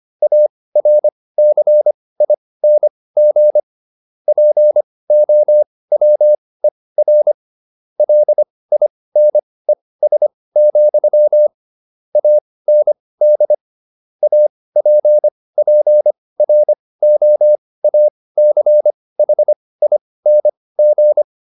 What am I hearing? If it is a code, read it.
Morse code, heard as a keyed tone, about 25 wpm, spaced as for 18 wpm, 600 Hz. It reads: ARCING POWER LINES, AND APPROACHING